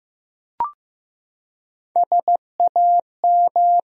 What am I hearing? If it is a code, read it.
Morse code, heard as a keyed tone, about 15 wpm, 700 Hz. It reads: SAM
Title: USA Common Names - Max 4 Letters - 15wpm